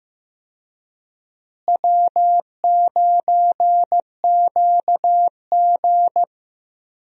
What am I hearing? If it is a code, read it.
Morse code, heard as a keyed tone, about 15 wpm, 700 Hz. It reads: W9QG
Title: USA Call Signs - 4 Characters - Repeated 2x - 15wpm